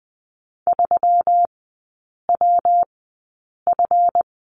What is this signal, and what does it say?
Morse code, heard as a keyed tone, about 20 wpm, spaced as for 8 wpm, 700 Hz. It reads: 3WF
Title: 205 Lesson 33e - Letter j - Three Characters 20wpm